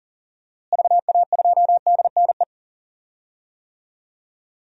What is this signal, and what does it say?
Morse code, heard as a keyed tone, about 40 wpm, 700 Hz. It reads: VA2DNE
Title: DXCC-100 Call Signs - 6 Characters - Repeated 2x - 40wpm